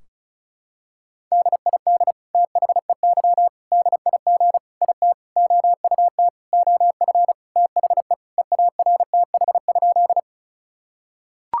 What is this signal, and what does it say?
Morse code, heard as a keyed tone, about 34 wpm, 700 Hz. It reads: DID THEY DIG IT OUT OF THE EARTH?
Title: Sentences from Top 1600 Words 35wpm